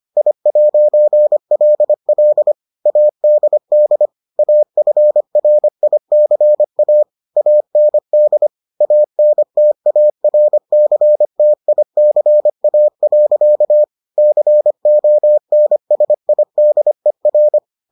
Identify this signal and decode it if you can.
Morse code, heard as a keyed tone, about 25 wpm, 600 Hz. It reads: I'LL ADD AFRICA AND ANTARCTICA. CONSIDER